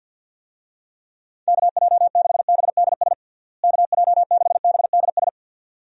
Morse code, heard as a keyed tone, about 51 wpm, 700 Hz. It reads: KJ6BDS KJ6BDS
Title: USA Call Signs - 6 Characters - Repeated 2x - 50wpm